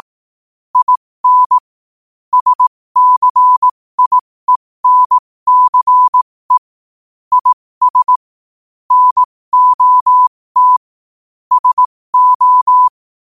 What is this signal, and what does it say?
Morse code, heard as a keyed tone, about 18 wpm, spaced as for 15 wpm, 1000 Hz. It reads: IN SCIENCE IS NOT SO
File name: qotdfiles / 15wpm